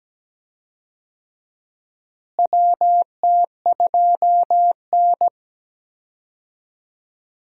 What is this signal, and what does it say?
Morse code, heard as a keyed tone, about 17 wpm, 700 Hz. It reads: WT2N
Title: USA Call Signs - 4 Characters - Repeated 2x - 17wpm